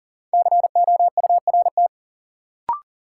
Morse code, heard as a keyed tone, about 40 wpm, 700 Hz. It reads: COURT
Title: Sentences from Top 1600 Words - New Word Drill Alt - 40wpm